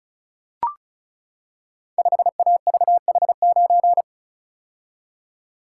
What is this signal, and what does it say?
Morse code, heard as a keyed tone, about 36 wpm, 700 Hz. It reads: 5AVH9
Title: Code Groups - 5 Letter:Numbers - 35wpm